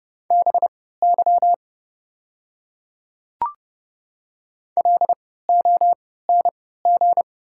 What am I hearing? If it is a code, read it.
Morse code, heard as a keyed tone, about 30 wpm, spaced as for 17 wpm, 700 Hz. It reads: BY LONG
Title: 3x Character Spacing - Top 100 Words - 30wpm